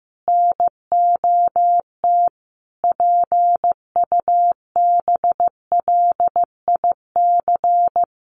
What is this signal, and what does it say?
Morse code, heard as a keyed tone, about 15 wpm, 700 Hz. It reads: NOT PUBLIC
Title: Sentences from Top 1200 Words 15wpm